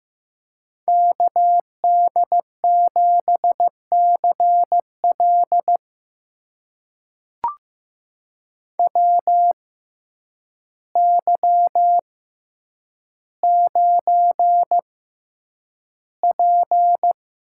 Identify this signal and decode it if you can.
Morse code, heard as a keyed tone, about 15 wpm, spaced as for 8 wpm, 700 Hz. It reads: KD7CL WY9P
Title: USA Call Signs - 6x Character Spacing - 15wpm